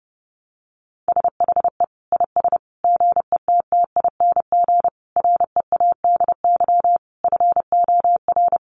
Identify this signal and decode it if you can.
Morse code, heard as a keyed tone, about 30 wpm, 700 Hz. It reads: SHE IS GETTING READY FOR
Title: Easy Phrases 30wpm